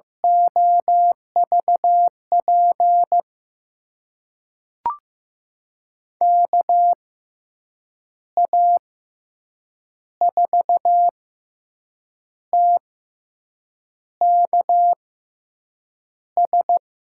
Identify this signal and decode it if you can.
Morse code, heard as a keyed tone, about 15 wpm, spaced as for 6 wpm, 700 Hz. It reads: OVP KA4TKS
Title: USA Call Signs - 6 Characters - 6x Character Spacing - 15wpm